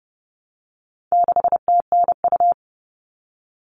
Morse code, heard as a keyed tone, about 30 wpm, 700 Hz. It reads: 6TNU
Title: All Cut Numbers - 4-Digit 30wpm